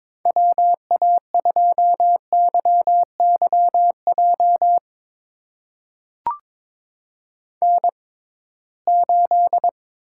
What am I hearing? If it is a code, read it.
Morse code, heard as a keyed tone, about 22 wpm, spaced as for 17 wpm, 700 Hz. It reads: WA2YYJ N8
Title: USA Call Signs - 6x Character Spacing - 22wpm